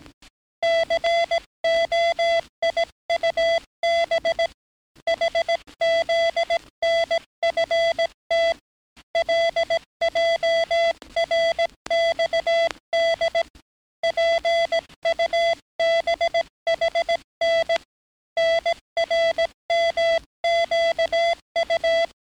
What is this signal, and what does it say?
Morse code, heard as a keyed tone, about 18 wpm, 670 Hz. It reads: COIUB HZNFT LJRXD PUBHN NRMQU
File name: training / NAVPERS 11049to11060 / NAVPERS 11058 RA RB